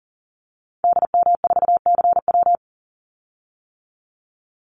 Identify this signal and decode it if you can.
Morse code, heard as a keyed tone, about 40 wpm, 700 Hz. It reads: DM4CW
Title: Call Signs - 40wpm